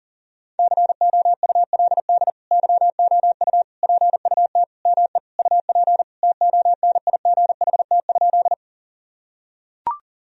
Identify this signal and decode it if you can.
Morse code, heard as a keyed tone, about 40 wpm, 700 Hz. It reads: COULD YOU PUT ME UP TONIGHT?
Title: Sentences from Top 1000 Words - Review Part 3 - 40wpm